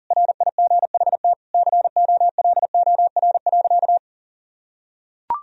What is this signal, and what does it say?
Morse code, heard as a keyed tone, about 40 wpm, 700 Hz. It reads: RIGHT COLOR.
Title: Sentences from Top 1000 Words - Review Part 2 - 40wpm